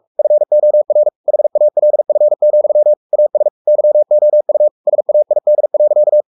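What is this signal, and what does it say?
Morse code, heard as a keyed tone, about 44 wpm, 600 Hz. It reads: FOR HALF, AS YOU SAID.
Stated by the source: Morse code at 43 wpm sending Ken Follett's novel Never in English, from CW med Gnister.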